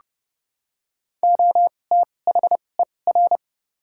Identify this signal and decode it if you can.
Morse code, heard as a keyed tone, about 30 wpm, spaced as for 22 wpm, 700 Hz. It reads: OTHER
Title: 2x Character Spacing - Top 100 Words - 30wpm